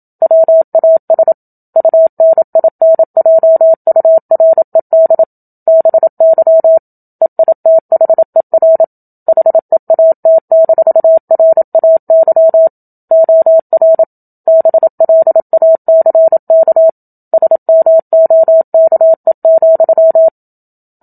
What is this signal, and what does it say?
Morse code, heard as a keyed tone, about 27 wpm, 650 Hz. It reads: WAS UNINJURED BY EITHER HEAT-RAY OR BLACK SMOKE,
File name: Book Two / 01-Chapter 1 - 27 WPM